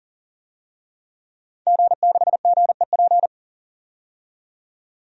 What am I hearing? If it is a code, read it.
Morse code, heard as a keyed tone, about 40 wpm, 700 Hz. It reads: G6GEP
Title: DXCC-100 Call Signs - 5 Characters - 40wpm